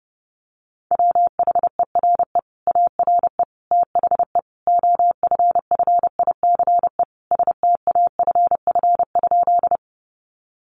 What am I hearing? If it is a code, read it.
Morse code, heard as a keyed tone, about 30 wpm, 700 Hz. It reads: WHERE ARE THE OFFICE STAFF?
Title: Sentences from Top 1200 Words 30wpm